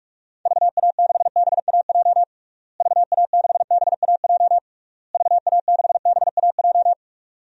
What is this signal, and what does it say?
Morse code, heard as a keyed tone, about 42 wpm, 700 Hz. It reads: VA6BAJ VA6BAJ VA6BAJ
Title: DXCC-100 Call Signs - 6 Characters - Repeated 3x - 45wpm